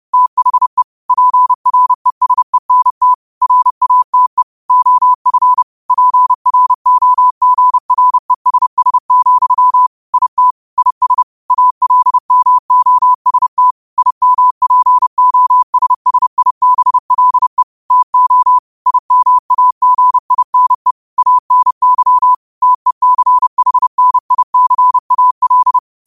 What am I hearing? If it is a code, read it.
Morse code, heard as a keyed tone, about 30 wpm, 1000 Hz. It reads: THE PRESENT RATE OF PROGRESS, IT IS ALMOST IMPOSSIBLE TO IMAGINE ANY TECHNICAL
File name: qotdfiles / 30wpm